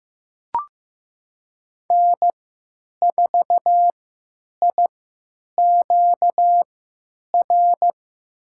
Morse code, heard as a keyed tone, about 15 wpm, spaced as for 8 wpm, 700 Hz. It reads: N4IQR
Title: USA Call Signs - 5 Characters - 3x Character Spacing - 15wpm